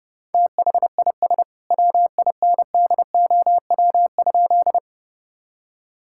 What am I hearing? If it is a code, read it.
Morse code, heard as a keyed tone, about 30 wpm, 700 Hz. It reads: THIS WINDOW?
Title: Sentences from Top 1600 Words 30wpm